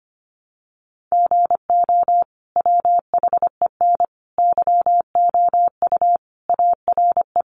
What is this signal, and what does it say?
Morse code, heard as a keyed tone, about 25 wpm, 700 Hz. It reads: GO WHEN YOU ARE
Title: Sentences from Top 500 Words 25wpm